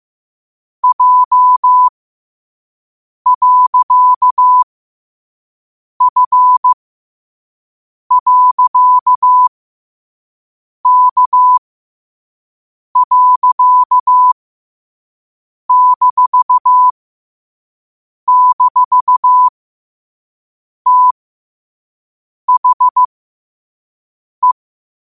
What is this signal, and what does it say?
Morse code, heard as a keyed tone, about 15 wpm, spaced as for 5 wpm, 1000 Hz. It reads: J.F.K.--THE
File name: qotdfiles / 5wpm